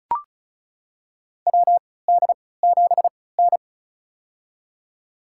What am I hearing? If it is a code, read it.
Morse code, heard as a keyed tone, about 36 wpm, spaced as for 20 wpm, 700 Hz. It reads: WD7N
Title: USA Call Signs - 4 Characters - 3x Character Spacing - 35wpm